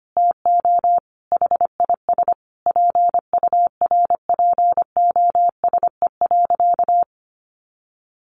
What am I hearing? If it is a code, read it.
Morse code, heard as a keyed tone, about 25 wpm, 700 Hz. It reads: TO HIS PURPOSE.
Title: Sentences from Top 1400 Words 25wpm